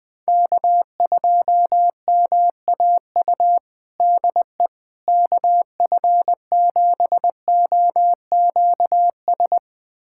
Morse code, heard as a keyed tone, about 20 wpm, 700 Hz. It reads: K2MAU DE KF7OQS
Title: USA Callsign DE Callsign 20wpm